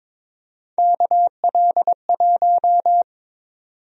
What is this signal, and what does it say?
Morse code, heard as a keyed tone, about 22 wpm, 700 Hz. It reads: KL1
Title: USA Call Sign Prefixes - 22wpm